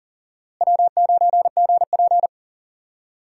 Morse code, heard as a keyed tone, about 40 wpm, 700 Hz. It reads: W9GP
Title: USA Call Signs - 4 Characters - Repeated 2x - 40wpm